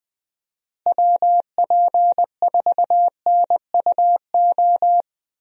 Morse code, heard as a keyed tone, about 20 wpm, 700 Hz. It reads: WP4NUO
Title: USA Call Signs - 6 Characters - Repeated 3x - 20wpm